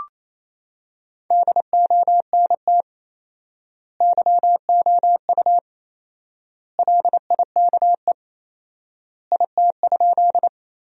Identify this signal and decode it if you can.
Morse code, heard as a keyed tone, about 27 wpm, 700 Hz. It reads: DONT YOU LIKE IT?
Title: 4x Word-Spacing - Sentences from Top 300 Words - 28wpm